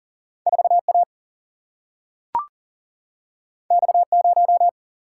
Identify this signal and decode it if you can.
Morse code, heard as a keyed tone, about 40 wpm, 700 Hz. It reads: 4A X0
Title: Single-Letter Single-Number 40wpm